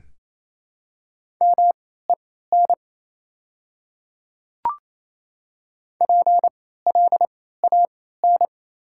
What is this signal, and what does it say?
Morse code, heard as a keyed tone, about 28 wpm, spaced as for 16 wpm, 700 Hz. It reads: MEN PLAN
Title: 3x Character Spacing - Top 500 Words - 28wpm